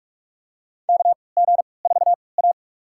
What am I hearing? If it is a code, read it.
Morse code, heard as a keyed tone, about 46 wpm, spaced as for 25 wpm, 700 Hz. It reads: KG4A
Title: USA Call Signs - 4 Characters - 3x Character Spacing - 45wpm